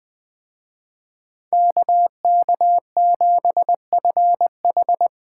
Morse code, heard as a keyed tone, about 20 wpm, 700 Hz. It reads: KK7FH
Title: Speed Racing - USA Call Signs 17wpm